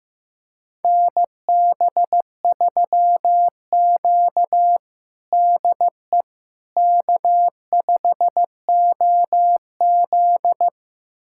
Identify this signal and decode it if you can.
Morse code, heard as a keyed tone, about 15 wpm, 700 Hz. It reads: NB3Q DE K5OZ
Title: USA Callsign DE Callsign - 4 Characters - 15wpm